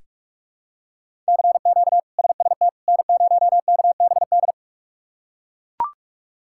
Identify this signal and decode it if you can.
Morse code, heard as a keyed tone, about 41 wpm, 700 Hz. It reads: CQ SST N0KBD
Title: K1USN SST QSO Practice - 45wpm